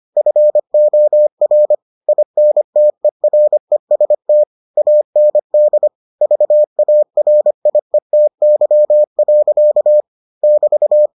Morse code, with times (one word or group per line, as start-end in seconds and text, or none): 0.2-1.7 FOR
2.1-4.4 INTEREST
4.8-5.9 AND
6.2-10.0 VARIETY.
10.4-11.1 <BT>